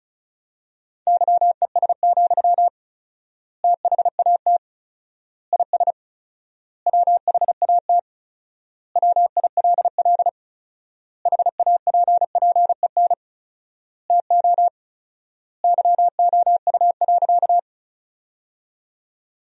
Morse code, heard as a keyed tone, about 34 wpm, 700 Hz. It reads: YES, THAT IS WHAT WILL HAPPEN TO YOU.
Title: 4x Word-Spacing - Sentences from Top 500 Words - 35wpm